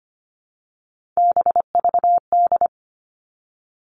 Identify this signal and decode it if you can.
Morse code, heard as a keyed tone, about 25 wpm, 700 Hz. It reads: BVD
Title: All Cut Numbers - 3-Digit 25wpm